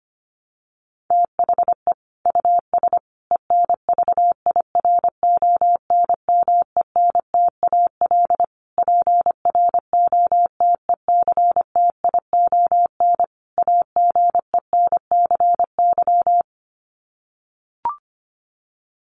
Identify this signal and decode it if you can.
Morse code, heard as a keyed tone, about 25 wpm, 700 Hz. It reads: THE US ENVIRONMENTAL PROTECTION AGENCY
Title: Sets of 5 Words Encore2 25wpm